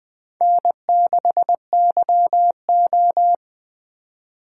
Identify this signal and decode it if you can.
Morse code, heard as a keyed tone, about 20 wpm, 700 Hz. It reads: N6YO
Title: USA Call Signs - 4 Characters - 2x Character Spacing - 20wpm